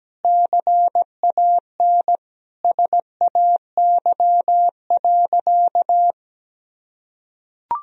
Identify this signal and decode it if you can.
Morse code, heard as a keyed tone, about 17 wpm, 700 Hz. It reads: CAN SAY.
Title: Sentences from Top 1000 Words - Review Part 2 - 17wpm